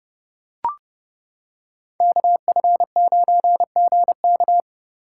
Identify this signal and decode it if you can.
Morse code, heard as a keyed tone, about 31 wpm, 700 Hz. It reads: KF9GK